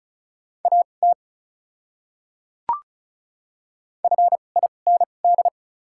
Morse code, heard as a keyed tone, about 36 wpm, spaced as for 25 wpm, 700 Hz. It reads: AT FIND
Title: 2x Character Spacing - Top 100 Words - 35wpm